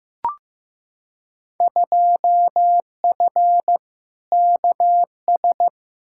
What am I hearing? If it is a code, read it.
Morse code, heard as a keyed tone, about 15 wpm, 700 Hz. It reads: 2F KS
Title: Field Day Exchanges Encore - 15wpm